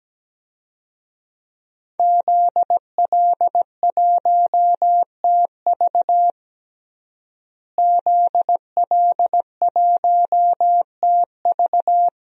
Morse code, heard as a keyed tone, about 17 wpm, 700 Hz. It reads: ZL1TV ZL1TV
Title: DXCC-100 Call Signs - 5 Characters - Repeated 3x - 17wpm